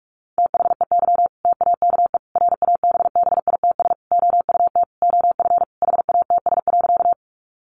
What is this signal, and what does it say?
Morse code, heard as a keyed tone, about 44 wpm, 700 Hz. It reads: THEY TAKE RABBITS OUT OF HATS.
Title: Easy Phrases 45wpm